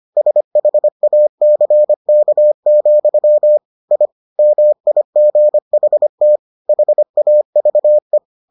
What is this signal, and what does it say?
Morse code, heard as a keyed tone, about 25 wpm, 600 Hz. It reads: SHACK, I MIGHT HAVE